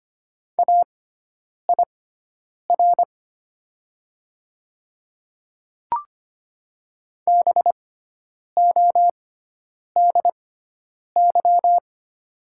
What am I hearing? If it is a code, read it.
Morse code, heard as a keyed tone, about 25 wpm, spaced as for 9 wpm, 700 Hz. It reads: AIR BODY
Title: 6x Character Spacing - Top 500 Words - 25wpm